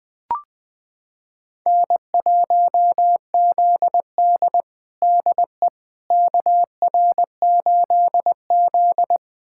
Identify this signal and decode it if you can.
Morse code, heard as a keyed tone, about 20 wpm, 700 Hz. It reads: N1ZD DE KR8Z
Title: USA Callsign DE Callsign - 4 Characters - 20wpm